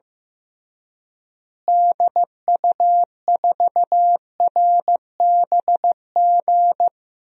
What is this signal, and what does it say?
Morse code, heard as a keyed tone, about 15 wpm, 700 Hz. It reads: DU4RBG
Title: DXCC-100 Call Signs - 6 Characters - Repeated 3x - 15wpm